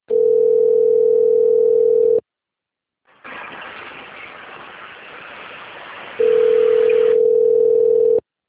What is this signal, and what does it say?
Ringback tone (North America), 440 + 480 Hz.